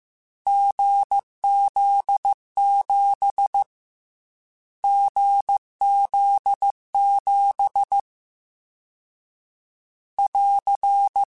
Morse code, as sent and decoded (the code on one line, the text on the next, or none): --. --.. --... / --. --.. --... / .-.-.
GZ7 GZ7 <AR>